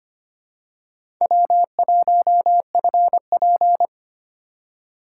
Morse code, heard as a keyed tone, about 25 wpm, 700 Hz. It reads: W1FP
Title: USA Call Signs - 4 Characters - 3x Character Spacing - 25wpm